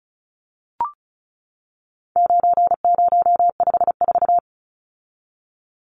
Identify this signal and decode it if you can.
Morse code, heard as a keyed tone, about 34 wpm, 700 Hz. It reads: 9054
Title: Numbers - 4-Digit 35wpm